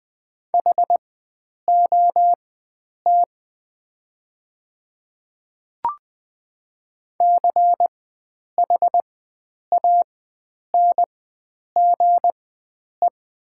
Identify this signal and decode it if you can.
Morse code, heard as a keyed tone, about 20 wpm, spaced as for 9 wpm, 700 Hz. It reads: HOT CHANGE
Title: 4x Character Spacing - Top 300 Words - 20wpm